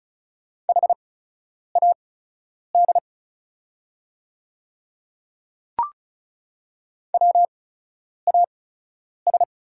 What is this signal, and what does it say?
Morse code, heard as a keyed tone, about 36 wpm, spaced as for 10 wpm, 700 Hz. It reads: HAD WAS
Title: 8x Character Spacing - Top 100 Words - 35wpm